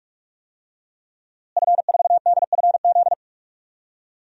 Morse code, heard as a keyed tone, about 46 wpm, 700 Hz. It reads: F4DFZ
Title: DXCC-100 Call Signs - 5 Characters - Repeated 2x - 45wpm